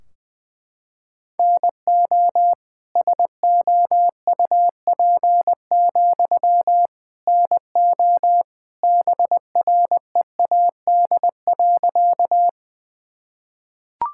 Morse code, heard as a keyed tone, about 20 wpm, 700 Hz. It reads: NO SOUP, NO BREAD.
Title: Sentences from Top 1600 Words 20wpm